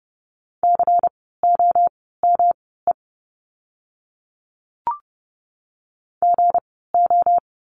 Morse code, heard as a keyed tone, about 30 wpm, spaced as for 17 wpm, 700 Hz. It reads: COME GO
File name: course / 30:10wpm / 237 Lesson 38f - Letter q - Words 30wpm